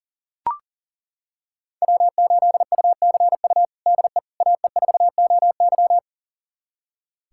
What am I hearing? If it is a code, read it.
Morse code, heard as a keyed tone, about 40 wpm, 700 Hz. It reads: W8UCU DE AE4OY
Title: USA Callsign DE Callsign 40wpm